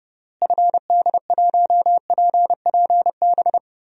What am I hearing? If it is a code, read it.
Morse code, heard as a keyed tone, about 30 wpm, 700 Hz. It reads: FD1PPB
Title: DXCC-100 Call Signs - 6 Characters - Repeated 3x - 30wpm